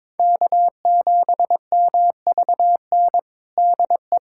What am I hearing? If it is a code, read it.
Morse code, heard as a keyed tone, about 22 wpm, 700 Hz. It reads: K7MVN DE
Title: USA Callsign DE Callsign 22wpm